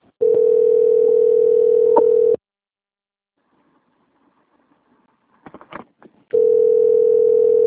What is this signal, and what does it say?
Ringback tone (North America), 440 + 480 Hz.